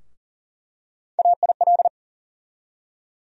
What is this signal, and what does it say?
Morse code, heard as a keyed tone, about 40 wpm, 700 Hz. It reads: AIL
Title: USA Call Sign Postfixes - 3 Letters 40wpm